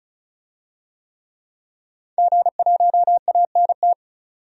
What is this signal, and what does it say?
Morse code, heard as a keyed tone, about 34 wpm, 700 Hz. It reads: G1ANT
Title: DXCC-100 Call Signs - 5 Characters - 35wpm